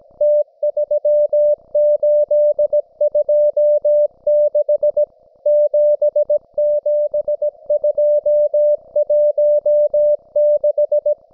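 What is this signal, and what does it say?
Morse code, heard as a keyed tone, about 17 wpm, 590 Hz. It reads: T3826 77216
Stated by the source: Russian military Morse station, M32, RDL, on 5268 kHz.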